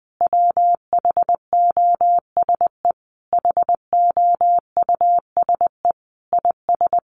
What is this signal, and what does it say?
Morse code, heard as a keyed tone, about 20 wpm, 700 Hz. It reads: WHOSE HOUSE IS
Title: Sentences from Top 1200 Words 20wpm